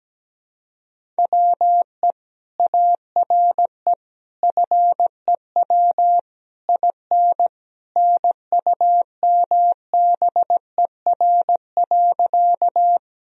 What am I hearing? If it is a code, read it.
Morse code, heard as a keyed tone, about 17 wpm, 700 Hz. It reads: WE ARE FEW IN NUMBER.
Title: Sentences from Top 1000 Words - Review Part 1 - 17wpm